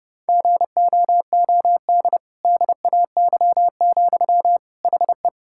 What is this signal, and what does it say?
Morse code, heard as a keyed tone, about 32 wpm, 700 Hz. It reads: GOOD DAY, HE
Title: Sentences from Top 1600 Words 30wpm